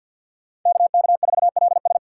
Morse code, heard as a keyed tone, about 48 wpm, 700 Hz. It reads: KK4LS
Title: USA Call Signs - 5 Characters - 4x Character Spacing - 50wpm